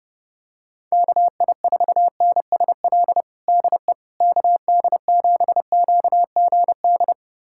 Morse code, heard as a keyed tone, about 30 wpm, 700 Hz. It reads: KI4NSL DE KD7QGD
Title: USA Callsign DE Callsign - 6 Characters - 30wpm